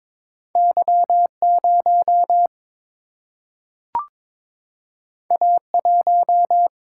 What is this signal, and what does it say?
Morse code, heard as a keyed tone, about 22 wpm, 700 Hz. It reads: Y0 A1
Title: Single-Letter Single-Number 22wpm